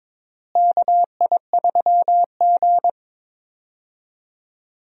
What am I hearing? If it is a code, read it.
Morse code, heard as a keyed tone, about 22 wpm, 700 Hz. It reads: KI3G
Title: USA Call Signs - 4 Characters - Repeated 3x - 22wpm